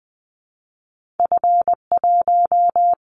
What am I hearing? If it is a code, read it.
Morse code, heard as a keyed tone, about 20 wpm, 700 Hz. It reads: F1
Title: Callsign Prefixes for Top 50 DXCC - 20wpm